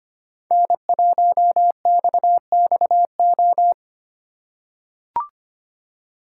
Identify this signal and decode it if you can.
Morse code, heard as a keyed tone, about 25 wpm, 700 Hz. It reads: N1XXO
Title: USA Call Signs - 5 Characters - 2x Character Spacing - 25wpm